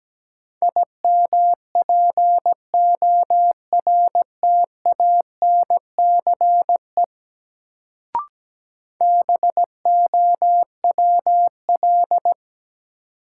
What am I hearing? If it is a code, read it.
Morse code, heard as a keyed tone, about 17 wpm, 700 Hz. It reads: IMPORTANCE BOWL
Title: Sentences from Top 1200 Words - New Word Drill Alt - 17wpm